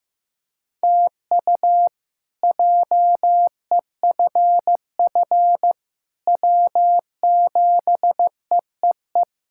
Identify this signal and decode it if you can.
Morse code, heard as a keyed tone, about 15 wpm, 700 Hz. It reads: TU JEFF W7EEE
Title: K1USN SST QSO Practice - 15wpm